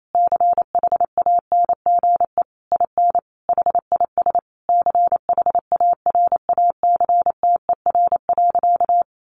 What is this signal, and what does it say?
Morse code, heard as a keyed tone, about 28 wpm, 700 Hz. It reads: CHANGE IN HIS CHARACTER.
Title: Sentences from Top 1200 Words 28wpm